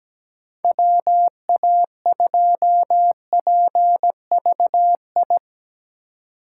Morse code, heard as a keyed tone, about 17 wpm, 700 Hz. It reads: WA2PVI
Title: USA Call Signs - 6 Characters - 2x Character Spacing - 17wpm